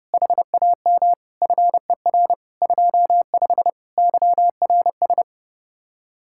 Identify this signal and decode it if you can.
Morse code, heard as a keyed tone, about 30 wpm, 700 Hz. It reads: HAM FER 25 YRS